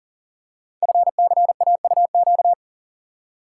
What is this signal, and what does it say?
Morse code, heard as a keyed tone, about 39 wpm, 700 Hz. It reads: FCAUQ
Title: Code Groups - 5 Letters - 40wpm